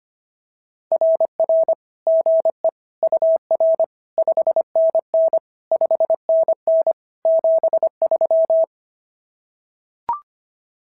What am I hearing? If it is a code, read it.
Morse code, heard as a keyed tone, about 25 wpm, 650 Hz. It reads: RR GE UR 5NN 5NN 73
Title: SOTA QSO Practice 25wpm